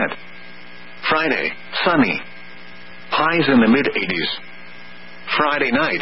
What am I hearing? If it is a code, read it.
Emergency Alert System (SAME) tones (header bursts), heard as harsh buzzing bursts, with speech. It is heard from 0.2 to 5.2 s.